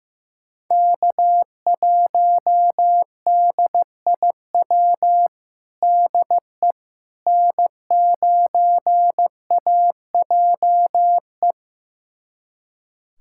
Morse code, heard as a keyed tone, about 16 wpm, 700 Hz. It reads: K1DIW DE N9AJE
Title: USA Callsign DE Callsign - 5 Characters - 15wpm